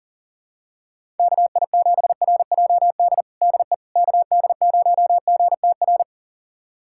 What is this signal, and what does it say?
Morse code, heard as a keyed tone, about 40 wpm, 700 Hz. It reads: KI7RJD DE KD0GTR
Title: USA Callsign DE Callsign - 6 Characters - 40wpm